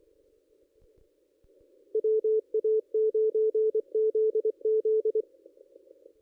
Morse code, heard as a keyed tone, about 23 wpm, 440 Hz. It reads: WA9ZZ
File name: contests / ss-cw-k5tr-op-n5rz -1900